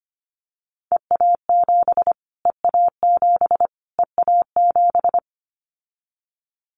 Morse code, heard as a keyed tone, about 25 wpm, 700 Hz. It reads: EA7 EA7 EA7
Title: Callsign Prefixes for Top 25 DXCC - Repeated 3x - 25wpm